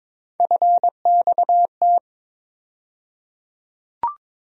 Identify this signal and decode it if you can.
Morse code, heard as a keyed tone, about 22 wpm, 700 Hz. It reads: FXT
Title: USA Call Sign Postfixes - 3 Letters 22wpm